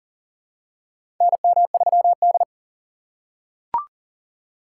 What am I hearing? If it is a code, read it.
Morse code, heard as a keyed tone, about 40 wpm, 700 Hz. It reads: NM3D
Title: USA Call Signs - 4x Character Spacing - 40wpm